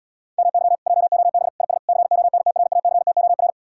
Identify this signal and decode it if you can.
Morse code, heard as a keyed tone, about 79 wpm, 700 Hz. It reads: MY JOB IS CONTRACTOR